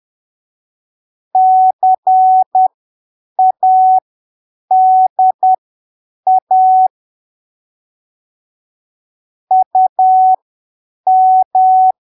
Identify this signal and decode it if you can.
Morse code, heard as a keyed tone, about 10 wpm, spaced as for 7 wpm, 750 Hz. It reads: CADA UM
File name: arquivos / texto / texto 20 10ppm